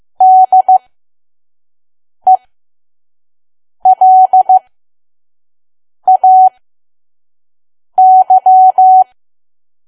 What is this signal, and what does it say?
Morse code, heard as a keyed tone, about 15 wpm, spaced as for 5 wpm, 750 Hz. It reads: DELAY
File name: Archive / 5 WPM files / PM